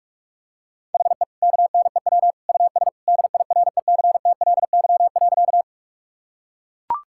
Morse code, heard as a keyed tone, about 41 wpm, 700 Hz. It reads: HE KNEW US DIRECTLY.
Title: Sentences from Top 1000 Words - Review Part 5 - 45wpm